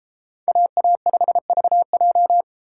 Morse code, heard as a keyed tone, about 33 wpm, 700 Hz. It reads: AA5VJ